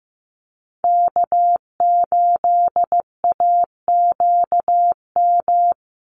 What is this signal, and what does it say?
Morse code, heard as a keyed tone, about 15 wpm, 700 Hz. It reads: K8AQM
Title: Call Signs Repeated 3x - 15wpm